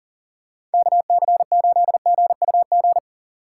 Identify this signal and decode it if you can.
Morse code, heard as a keyed tone, about 40 wpm, 700 Hz. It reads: KC8GUG
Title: USA Call Signs - 6 Characters - 6x Character Spacing - 40wpm